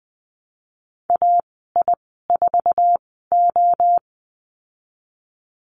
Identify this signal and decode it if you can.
Morse code, heard as a keyed tone, about 20 wpm, 700 Hz. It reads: A I 4 O
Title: 062 Lesson 11g - Letter h - Call Signs 20wpm